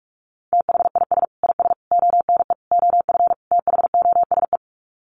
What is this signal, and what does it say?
Morse code, heard as a keyed tone, about 44 wpm, 700 Hz. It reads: THIS IS ONE OF THOSE
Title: Sets of 5 Words Encore2 45wpm